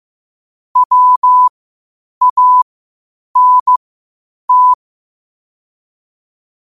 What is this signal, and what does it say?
Morse code, heard as a keyed tone, about 15 wpm, spaced as for 8 wpm, 1000 Hz. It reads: WANT
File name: qotdfiles / 8wpm